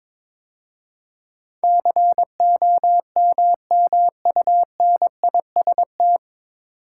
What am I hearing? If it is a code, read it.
Morse code, heard as a keyed tone, about 23 wpm, 700 Hz. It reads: COMMUNIST
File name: 9 Letter Words - First 3 Letters Given - 22wpm